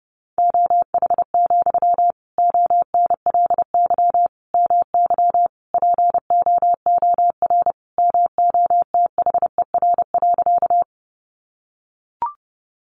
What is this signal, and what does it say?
Morse code, heard as a keyed tone, about 30 wpm, 700 Hz. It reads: OH, ONLY MY POOR MOTHER.
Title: Sentences from Top 1000 Words 30wpm